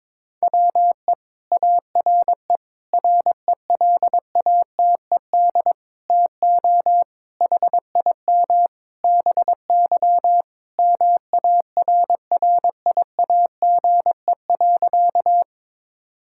Morse code, heard as a keyed tone, about 22 wpm, 700 Hz. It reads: WE ARE RELATED TO HIM BY MARRIAGE.
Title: Sentences from Top 1000 Words - Review Part 5 - 22wpm